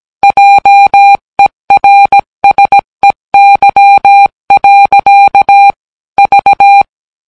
Morse code, heard as a keyed tone, about 17 wpm, 800 Hz. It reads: JERSEY. V